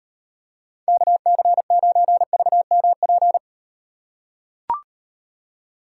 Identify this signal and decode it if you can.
Morse code, heard as a keyed tone, about 38 wpm, 700 Hz. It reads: KC9VMP